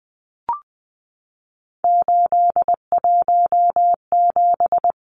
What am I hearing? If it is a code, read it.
Morse code, heard as a keyed tone, about 20 wpm, 700 Hz. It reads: 817
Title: Numbers - 3-Digit 20wpm